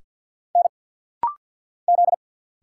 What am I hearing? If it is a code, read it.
Morse code, heard as a keyed tone, about 50 wpm, 700 Hz. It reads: N Z
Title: Single Letter-Number Alt - Rapid-Fire 50wpm